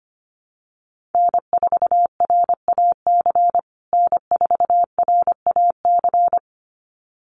Morse code, heard as a keyed tone, about 25 wpm, 700 Hz. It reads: N4RAC N4RAC
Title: USA Call Signs Repeated 2x 25wpm